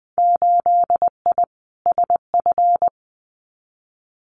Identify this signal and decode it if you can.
Morse code, heard as a keyed tone, about 20 wpm, 700 Hz. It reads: 8I SF
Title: Winter Field Day Exchanges Encore 20wpm